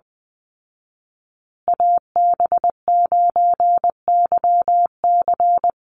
Morse code, heard as a keyed tone, about 20 wpm, 700 Hz. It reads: AB9YC
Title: Call Signs Repeated 2x - 20wpm